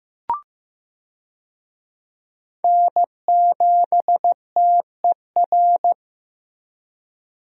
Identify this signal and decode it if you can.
Morse code, heard as a keyed tone, about 15 wpm, 700 Hz. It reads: N7TER
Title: USA Call Signs - 5 Characters - Repeated 3x - 15wpm